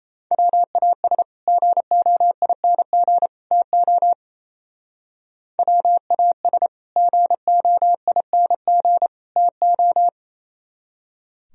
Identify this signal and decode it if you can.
Morse code, heard as a keyed tone, about 36 wpm, 700 Hz. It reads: WAS GOING TO WAS GOING TO